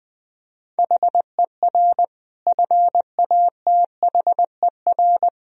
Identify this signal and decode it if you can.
Morse code, heard as a keyed tone, about 21 wpm, 700 Hz. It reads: HER FATHER
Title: Sentences from Top 1000 Words - Review Part 3 - 20wpm